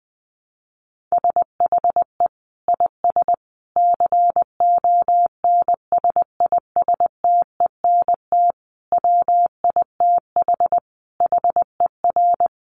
Morse code, heard as a keyed tone, about 20 wpm, 700 Hz. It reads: SHE IS CONSISTENT WITH HER
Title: Sentences from Top 1500 Words 20wpm